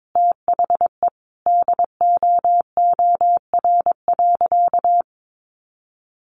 Morse code, heard as a keyed tone, about 22 wpm, 700 Hz. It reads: THE DOOR.
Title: Sentences from Top 1200 Words 22wpm